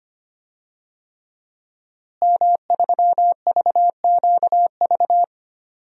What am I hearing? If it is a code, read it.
Morse code, heard as a keyed tone, about 25 wpm, 700 Hz. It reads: M3VQV
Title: DXCC-100 Call Signs - 5 Characters - Repeated 3x - 25wpm